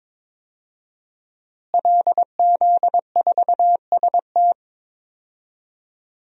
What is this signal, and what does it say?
Morse code, heard as a keyed tone, about 22 wpm, 700 Hz. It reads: LZ4ST
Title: DXCC-100 Call Signs - 5 Characters - Repeated 2x - 22wpm